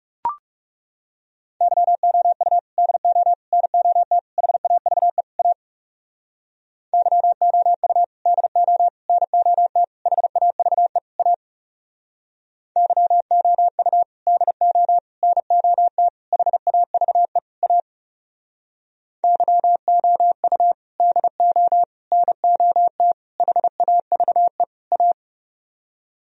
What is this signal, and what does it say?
Morse code, heard as a keyed tone, about 38 wpm, 700 Hz. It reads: YOU DO NOT HAVE A YOU DO NOT HAVE A YOU DO NOT HAVE A YOU DO NOT HAVE A